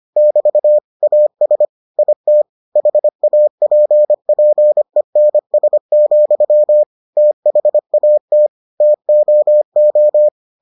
Morse code, heard as a keyed tone, about 25 wpm, 600 Hz. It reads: <BT> AS IT HAPPENS, THAT TOO